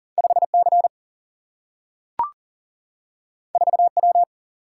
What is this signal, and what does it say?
Morse code, heard as a keyed tone, about 39 wpm, 700 Hz. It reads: 5 C 4 W
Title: Single-Letter Single-Number 40wpm